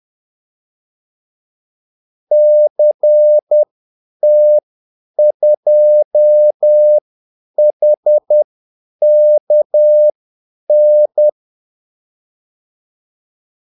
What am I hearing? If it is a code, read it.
Morse code, heard as a keyed tone, about 10 wpm, spaced as for 8 wpm, 600 Hz. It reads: CT2HKN
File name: Callsigns-10-8wpm